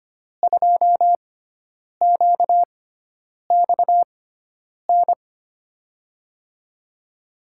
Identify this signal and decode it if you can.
Morse code, heard as a keyed tone, about 25 wpm, spaced as for 9 wpm, 700 Hz. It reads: 2QXN
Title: USA Call Signs - 5 Characters - 6x Character Spacing - 25wpm